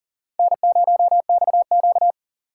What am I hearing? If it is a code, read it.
Morse code, heard as a keyed tone, about 51 wpm, 700 Hz. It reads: N0XQ